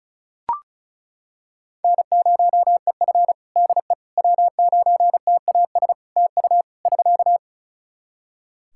Morse code, heard as a keyed tone, about 36 wpm, 700 Hz. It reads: N0EF DE W9TAS TU <SK>
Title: Practice QSO - Exchange 4 - Ending - 35wpm